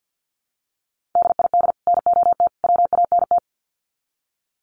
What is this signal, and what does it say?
Morse code, heard as a keyed tone, about 50 wpm, 700 Hz. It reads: DID NOT WANT
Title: Sets of 3 Words 50wpm